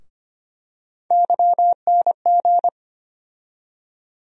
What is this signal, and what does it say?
Morse code, heard as a keyed tone, about 25 wpm, 700 Hz. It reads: YNG
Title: USA Call Sign Postfixes - 3 Letters 25wpm